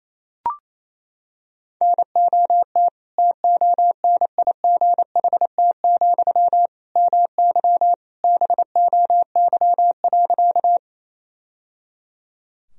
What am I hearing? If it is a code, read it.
Morse code, heard as a keyed tone, about 28 wpm, 700 Hz. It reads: NOT TONIGHT, MY BOY.
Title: Sentences from Top 1000 Words - Review Part 3 - 28wpm